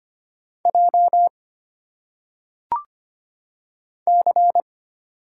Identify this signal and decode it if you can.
Morse code, heard as a keyed tone, about 25 wpm, 700 Hz. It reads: J C
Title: Single Letters - Phonetic Pronunciation 25wpm